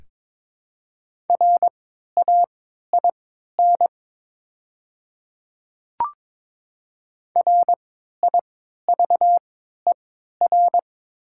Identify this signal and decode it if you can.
Morse code, heard as a keyed tone, about 22 wpm, spaced as for 12 wpm, 700 Hz. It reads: RAIN RIVER
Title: 3x Character Spacing - Top 300 Words - 22wpm